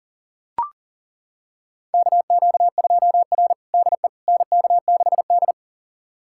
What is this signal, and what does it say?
Morse code, heard as a keyed tone, about 37 wpm, 700 Hz. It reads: KQ2R DE NK6D
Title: USA Callsign DE Callsign - 4 Characters - 40wpm